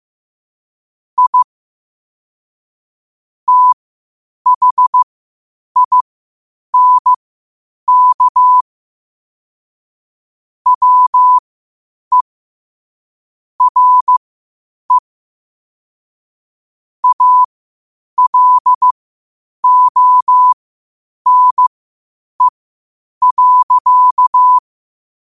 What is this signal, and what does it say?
Morse code, heard as a keyed tone, about 15 wpm, spaced as for 8 wpm, 1000 Hz. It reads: I THINK WE RE ALONE.
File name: qotdfiles / 8wpm